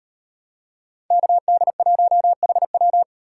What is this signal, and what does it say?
Morse code, heard as a keyed tone, about 38 wpm, 700 Hz. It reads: KD1HW